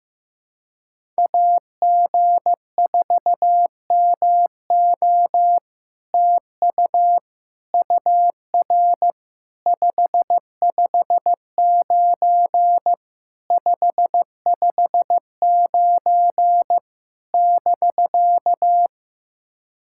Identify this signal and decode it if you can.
Morse code, heard as a keyed tone, about 15 wpm, 700 Hz. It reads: AG4MO TU UR 559 559 <BK>
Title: SOTA QSO Practice 15wpm